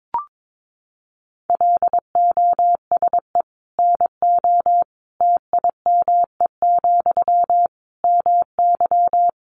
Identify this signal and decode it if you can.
Morse code, heard as a keyed tone, about 22 wpm, 700 Hz. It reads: LOSE NO TIME, MY
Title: Sentences from Top 1100 Words 22wpm